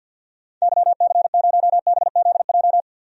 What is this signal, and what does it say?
Morse code, heard as a keyed tone, about 52 wpm, 700 Hz. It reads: YC0BZJ
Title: DXCC-100 Call Signs - 6 Characters - 50wpm